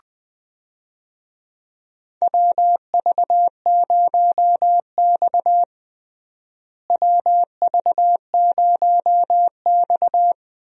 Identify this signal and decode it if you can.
Morse code, heard as a keyed tone, about 21 wpm, 700 Hz. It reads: WV0X WV0X
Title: USA Call Signs - 4 Characters - Repeated 2x - 20wpm